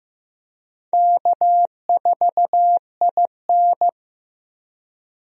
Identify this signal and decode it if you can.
Morse code, heard as a keyed tone, about 15 wpm, 700 Hz. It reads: K4IN
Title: USA Call Signs - 4 Characters - Repeated 3x - 15wpm